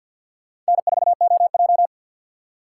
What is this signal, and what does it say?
Morse code, heard as a keyed tone, about 49 wpm, 700 Hz. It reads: N4OJ
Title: USA Call Signs - 4 Characters - 2x Character Spacing - 50wpm